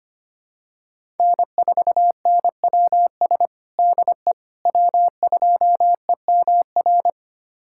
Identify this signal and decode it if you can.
Morse code, heard as a keyed tone, about 25 wpm, 700 Hz. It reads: N4NWS DE W2EMR
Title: USA Callsign DE Callsign - 5 Characters - 25wpm